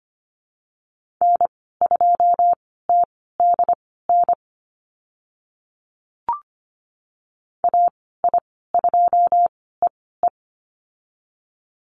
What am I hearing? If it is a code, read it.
Morse code, heard as a keyed tone, about 25 wpm, spaced as for 16 wpm, 700 Hz. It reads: N2TDN AI2EE
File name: Lesson 15g - Number 5 - Call Signs 25wpm